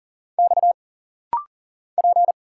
Morse code, heard as a keyed tone, about 40 wpm, 700 Hz. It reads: X P
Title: Single Letters - Rapid-Fire 40wpm